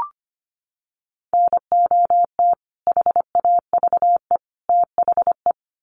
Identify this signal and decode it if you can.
Morse code, heard as a keyed tone, about 25 wpm, 700 Hz. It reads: NOT HAVE THE
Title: Sets of 3 Words 25wpm